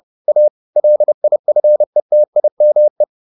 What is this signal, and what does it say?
Morse code, heard as a keyed tone, about 30 wpm, 600 Hz. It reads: A LIFETIME